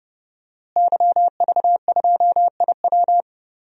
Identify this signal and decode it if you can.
Morse code, heard as a keyed tone, about 30 wpm, 700 Hz. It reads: YV2IW
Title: DXCC-100 Call Signs - 5 Characters - Repeated 2x - 30wpm